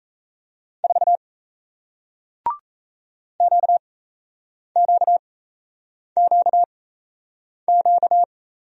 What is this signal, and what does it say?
Morse code, heard as a keyed tone, about 38 wpm, spaced as for 8 wpm, 700 Hz. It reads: 4 QQQQ